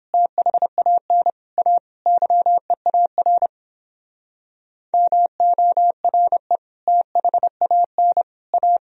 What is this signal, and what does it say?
Morse code, heard as a keyed tone, about 29 wpm, 700 Hz. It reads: THAN A YEAR MORE THAN A